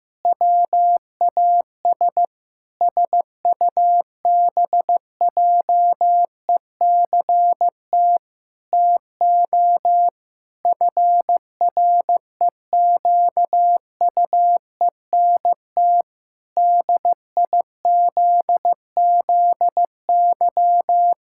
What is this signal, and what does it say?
Morse code, heard as a keyed tone, about 15 wpm, 700 Hz. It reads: WAS SUBJECT TO FREQUENT DIZZY
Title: Pangram Challenge v2 - 15wpm